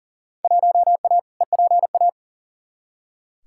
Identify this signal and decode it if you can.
Morse code, heard as a keyed tone, about 40 wpm, 700 Hz. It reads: 1A EPA
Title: Field Day Exchanges - 40wpm